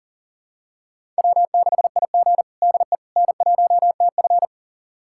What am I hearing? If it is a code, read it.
Morse code, heard as a keyed tone, about 40 wpm, 700 Hz. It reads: W6IG DE N1TF
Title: USA Callsign DE Callsign - 4 Characters - 40wpm